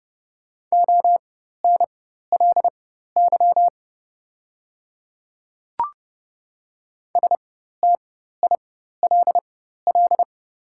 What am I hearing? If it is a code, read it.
Morse code, heard as a keyed tone, about 30 wpm, spaced as for 14 wpm, 700 Hz. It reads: ONLY STILL
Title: 4x Character Spacing - Top 300 Words - 30wpm